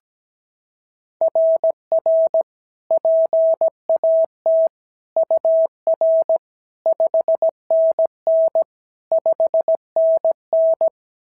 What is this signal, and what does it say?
Morse code, heard as a keyed tone, about 16 wpm, 650 Hz. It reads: RR PAT UR 5NN 5NN
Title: SOTA QSO Practice 17wpm